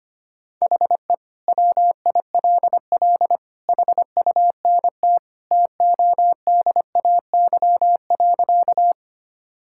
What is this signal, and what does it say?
Morse code, heard as a keyed tone, about 25 wpm, 700 Hz. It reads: HE WILL HUNT TODAY.
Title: Sentences from Top 1600 Words 25wpm